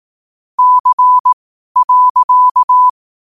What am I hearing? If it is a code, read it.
Morse code, heard as a keyed tone, about 18 wpm, spaced as for 12 wpm, 1000 Hz. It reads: C.
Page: qotd-at-12-wpm